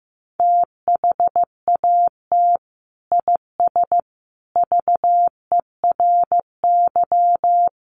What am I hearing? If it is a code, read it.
Morse code, heard as a keyed tone, about 15 wpm, 700 Hz. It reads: THAT IS VERY
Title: Sentences from Top 1000 Words 15wpm